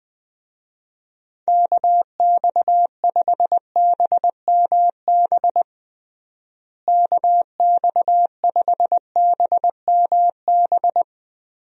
Morse code, heard as a keyed tone, about 21 wpm, 700 Hz. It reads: KX5BMB KX5BMB
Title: USA Call Signs - 6 Characters - Repeated 2x - 20wpm